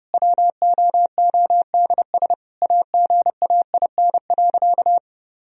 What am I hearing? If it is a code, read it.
Morse code, heard as a keyed tone, about 30 wpm, 700 Hz. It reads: WOODS AGAIN.
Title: Sentences from Top 1000 Words - Review Part 2 - 30wpm